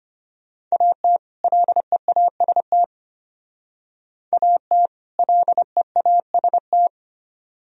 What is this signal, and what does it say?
Morse code, heard as a keyed tone, about 32 wpm, 700 Hz. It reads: AT LEAST AT LEAST